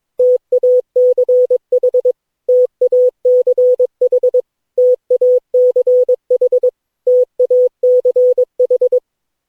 Morse code, heard as a keyed tone, about 22 wpm, 500 Hz. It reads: TACH TACH TACH TACH
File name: cw / Session9 cw training